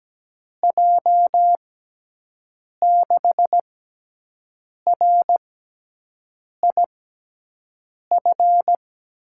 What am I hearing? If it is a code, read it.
Morse code, heard as a keyed tone, about 17 wpm, spaced as for 6 wpm, 700 Hz. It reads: J6RIF
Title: USA Call Signs - 6 Characters - 6x Character Spacing - 17wpm